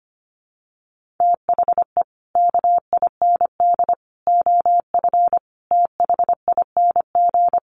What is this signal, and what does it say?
Morse code, heard as a keyed tone, about 25 wpm, 700 Hz. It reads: THE KIND OF THING
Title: Sets of 5 Words Encore 25wpm